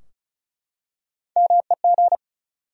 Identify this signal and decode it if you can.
Morse code, heard as a keyed tone, about 36 wpm, 700 Hz. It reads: MEG
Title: USA Common Names - Max 5 Letters - 35wpm